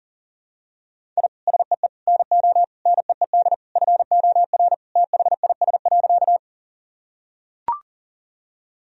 Morse code, heard as a keyed tone, about 40 wpm, 700 Hz. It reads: I SEE NO NEED FOR THIS.
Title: Sentences from Top 1000 Words - Review Part 1 - 40wpm